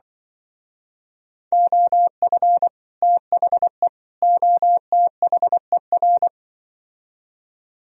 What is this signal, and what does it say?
Morse code, heard as a keyed tone, about 24 wpm, 700 Hz. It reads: OF THE OTHER
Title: Speed-Racing - Sets of 3 Words 20wpm